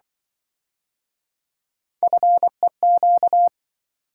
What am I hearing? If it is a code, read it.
Morse code, heard as a keyed tone, about 26 wpm, 700 Hz. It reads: FEQ